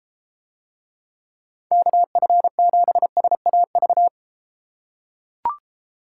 Morse code, heard as a keyed tone, about 32 wpm, 700 Hz. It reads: KF7SAV